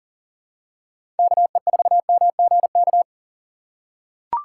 Morse code, heard as a keyed tone, about 40 wpm, 700 Hz. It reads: KE4MGK
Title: USA Call Signs - 6 Characters - 4x Character Spacing - 40wpm